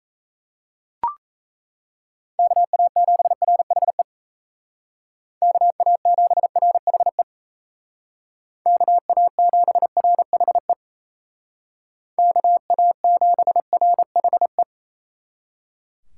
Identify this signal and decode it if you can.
Morse code, heard as a keyed tone, about 35 wpm, 700 Hz. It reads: KA7RHE KA7RHE KA7RHE KA7RHE
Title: Speed Racing - USA 2x3 Call Signs 28wpm